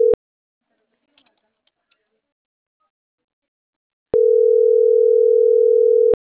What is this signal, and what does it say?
Ringback tone (North America), 440 + 480 Hz.